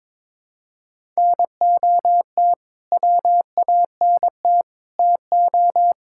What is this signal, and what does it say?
Morse code, heard as a keyed tone, about 22 wpm, 700 Hz. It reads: NOT WANT TO
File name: practice / 22wpm / Build A Phrase - 5-Word Phrases - 22wpm